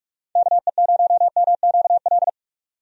Morse code, heard as a keyed tone, about 45 wpm, 700 Hz. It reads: KE0MQL
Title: USA Call Signs - 6 Characters - 45wpm